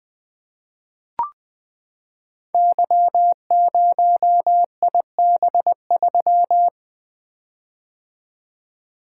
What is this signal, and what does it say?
Morse code, heard as a keyed tone, about 20 wpm, 700 Hz. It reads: Y0IB3
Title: Code Groups - 5 Letter:Numbers - 20wpm